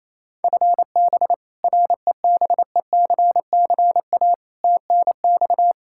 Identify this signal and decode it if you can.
Morse code, heard as a keyed tone, about 28 wpm, 700 Hz. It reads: FB REBECCA TNX
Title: Practice QSO - Exchange 3 - More Info - 28wpm